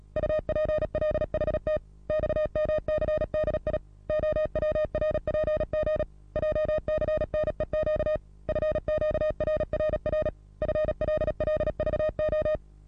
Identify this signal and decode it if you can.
Morse code, heard as a keyed tone, about 36 wpm, 620 Hz. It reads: UPLHT XMCDI OWRPG JCNEQ FQRRR FLLVO
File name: repos / Morsekurs / DJ4UF Morsen27